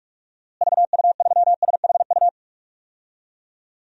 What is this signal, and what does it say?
Morse code, heard as a keyed tone, about 41 wpm, 700 Hz. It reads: VU3SHU
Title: DXCC-100 Call Signs - 6 Characters - Repeated 2x - 45wpm